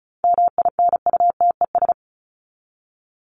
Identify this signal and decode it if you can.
Morse code, heard as a keyed tone, about 34 wpm, 700 Hz. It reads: MINUTES
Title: ES:S Suffix - Top 100 Words with Base Words 35wpm